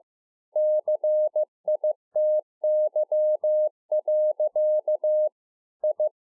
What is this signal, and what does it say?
Morse code, heard as a keyed tone, about 15 wpm, 610 Hz. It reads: CITY. I